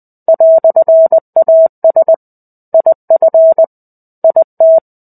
Morse code, heard as a keyed tone, about 20 wpm, 650 Hz. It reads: "AS IF IT
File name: 05-Chapter-05 20 WPM